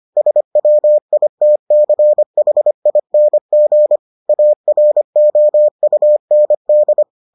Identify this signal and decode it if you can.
Morse code, heard as a keyed tone, about 25 wpm, 600 Hz. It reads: SWITCHING AROUND